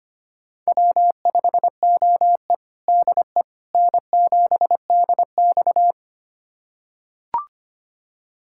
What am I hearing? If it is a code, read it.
Morse code, heard as a keyed tone, about 25 wpm, 700 Hz. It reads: W5OE DE N7DX
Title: USA Callsign DE Callsign - 4 Characters - 25wpm